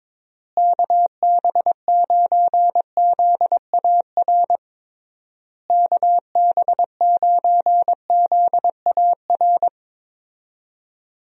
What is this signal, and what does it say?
Morse code, heard as a keyed tone, about 22 wpm, 700 Hz. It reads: KB9ZAR KB9ZAR
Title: USA Call Signs - 6 Characters - Repeated 3x - 22wpm